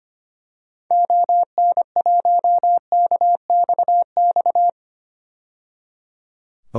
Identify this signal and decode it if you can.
Morse code, heard as a keyed tone, about 25 wpm, 700 Hz. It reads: ON1KXX
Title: DXCC-100 Call Signs - 6 Characters - 25wpm